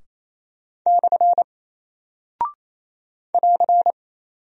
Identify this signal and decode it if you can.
Morse code, heard as a keyed tone, about 28 wpm, 700 Hz. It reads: / <AR>
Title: Single Character (Letter:Number:Punctuation:Prosign) Alt 28wpm